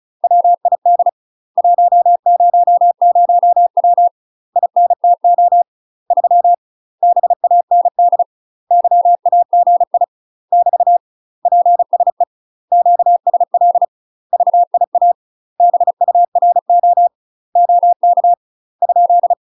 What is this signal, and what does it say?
Morse code, heard as a keyed tone, about 35 wpm, 700 Hz. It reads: WID 100W INTO 3 BAND YAGI <BT> PSE QSL VIA BURO OK ?
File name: UK MASTER QSOs TRAINING / 35WPM / UK MASTER QSOs 0005